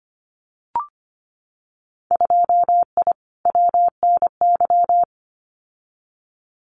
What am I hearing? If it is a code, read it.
Morse code, heard as a keyed tone, about 25 wpm, 690 Hz. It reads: 2I WNY
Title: Winter Field Day Exchanges Encore 25wpm